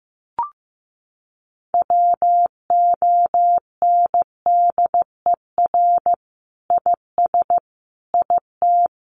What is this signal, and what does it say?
Morse code, heard as a keyed tone, about 15 wpm, 700 Hz. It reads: WONDER IS IT
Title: Sentences from Top 500 Words 15wpm